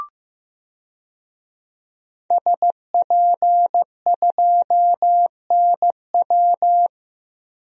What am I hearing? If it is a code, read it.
Morse code, heard as a keyed tone, about 15 wpm, 700 Hz. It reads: SP2NW
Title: DXCC-100 Call Signs - 5 Characters - Repeated 2x - 15wpm